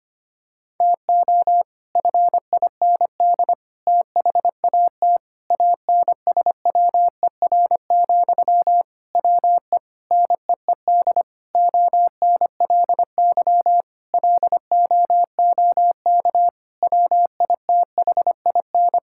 Morse code, heard as a keyed tone, about 25 wpm, 700 Hz. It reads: TO FIND THAT ANSWER, WE NEED ONLY LOOK WITHIN
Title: Sentences from Top 1600 Words 25wpm